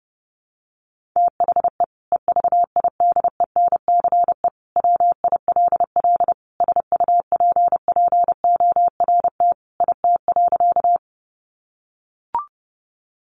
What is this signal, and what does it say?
Morse code, heard as a keyed tone, about 30 wpm, 700 Hz. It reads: THE EVIDENCE WILL SUPPORT IT.
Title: Sentences from Top 1300 Words 30wpm